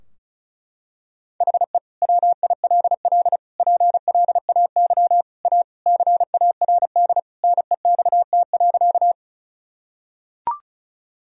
Morse code, heard as a keyed tone, about 36 wpm, 700 Hz. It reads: HE WILL PLAY A CARD NEXT.
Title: Sentences from Top 1000 Words - Review Part 4 - 35wpm